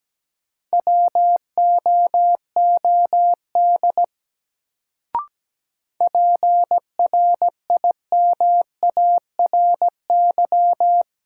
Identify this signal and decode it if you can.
Morse code, heard as a keyed tone, about 17 wpm, 700 Hz. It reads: WOOD PRIMARY
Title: Sentences from Top 1100 Words - New Word Drill Alt - 17wpm